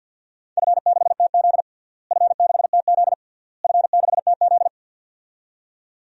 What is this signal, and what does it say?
Morse code, heard as a keyed tone, about 47 wpm, 700 Hz. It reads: F6TZ F6TZ F6TZ
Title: DXCC-100 Call Signs - 4 Characters - Repeated 3x - 50wpm